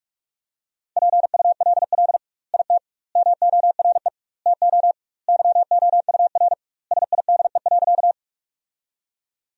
Morse code, heard as a keyed tone, about 42 wpm, 700 Hz. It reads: PULL IT MORE TO YOUR SIDE.
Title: 2x Word-Spacing - Sentences from Top 400 Words - 45wpm